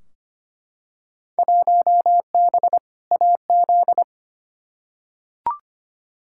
Morse code, heard as a keyed tone, about 25 wpm, 700 Hz. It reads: 1B AZ
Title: Field Day Exchanges Encore - 25wpm